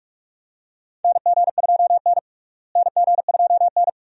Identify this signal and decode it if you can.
Morse code, heard as a keyed tone, about 46 wpm, 700 Hz. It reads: NG2N NG2N
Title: USA Call Signs - 4 Characters - Repeated 2x - 45wpm